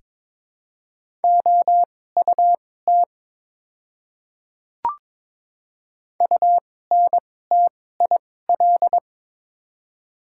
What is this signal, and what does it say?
Morse code, heard as a keyed tone, about 22 wpm, spaced as for 16 wpm, 700 Hz. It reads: OUT UNTIL
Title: 2x Character Spacing - Top 300 Words - 22wpm